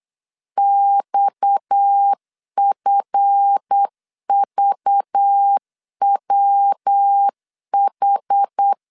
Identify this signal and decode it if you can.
Morse code, heard as a keyed tone, about 8 wpm, 800 Hz. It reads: XFVWH
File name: LEZIONI / 8 YJ45 / 8 YJ 8wpm 3313